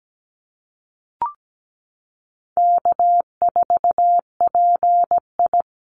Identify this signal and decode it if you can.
Morse code, heard as a keyed tone, about 17 wpm, 700 Hz. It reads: K4PI
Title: Easy Call Signs Repeated 3x - 17wpm